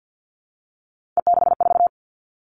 Morse code, heard as a keyed tone, about 50 wpm, 700 Hz. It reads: E64
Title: Three Challenging Letter-Number Combinations 50wpm